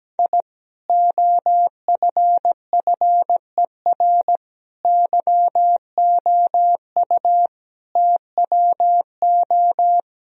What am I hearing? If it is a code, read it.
Morse code, heard as a keyed tone, about 17 wpm, 700 Hz. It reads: I OFFER YOU TWO
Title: Sentences from Top 1000 Words - Review Part 4 - 17wpm